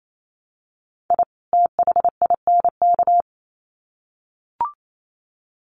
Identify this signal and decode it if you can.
Morse code, heard as a keyed tone, about 28 wpm, 700 Hz. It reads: I THINK
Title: Sets of 2 Words 28wpm